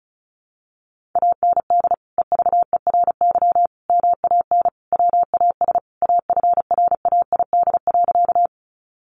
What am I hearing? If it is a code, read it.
Morse code, heard as a keyed tone, about 34 wpm, 700 Hz. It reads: AND EVERY MAN WAS AFRAID.
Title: Sentences from Top 1500 Words 35wpm